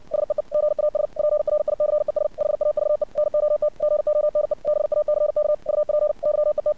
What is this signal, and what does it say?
Morse code, heard as a keyed tone, about 64 wpm, 600 Hz. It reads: SEE GAS PRICES HAVE NOT GONE BACK UP YET
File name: archived / CRSnet